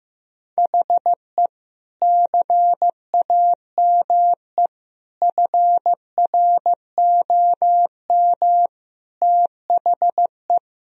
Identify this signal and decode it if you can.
Morse code, heard as a keyed tone, about 16 wpm, 700 Hz. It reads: HE CAME FROM THE
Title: Sentences from Top 1000 Words - Review Part 2 - 15wpm